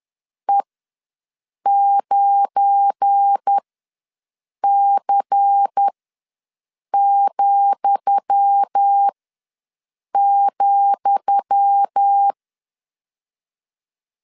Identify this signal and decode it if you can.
Morse code, heard as a keyed tone, about 10 wpm, spaced as for 6 wpm, 800 Hz. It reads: E9C,,
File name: LEZIONI / 10 SIGNS / 10D COMA ws 10wpm 4314